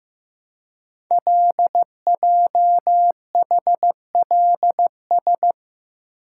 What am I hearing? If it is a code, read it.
Morse code, heard as a keyed tone, about 15 wpm, 700 Hz. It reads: LJHLS
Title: Code Groups - 5 Letters - 15wpm